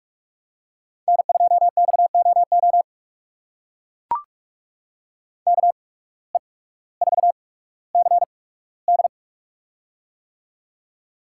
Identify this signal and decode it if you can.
Morse code, heard as a keyed tone, about 43 wpm, spaced as for 19 wpm, 700 Hz. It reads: N2XOO KE4CD
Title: USA Call Signs - 5 Characters - 8x Character Spacing - 45wpm